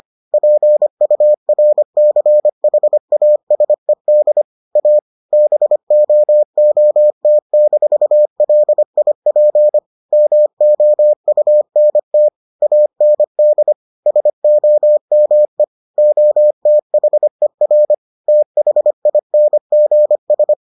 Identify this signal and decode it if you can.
Morse code, heard as a keyed tone, about 25 wpm, 600 Hz. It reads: PURCHASED A BOOT-LIP MOUNT AND SOME OTHER THINGS